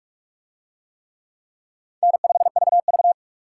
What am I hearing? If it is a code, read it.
Morse code, heard as a keyed tone, about 40 wpm, 700 Hz. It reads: N5VV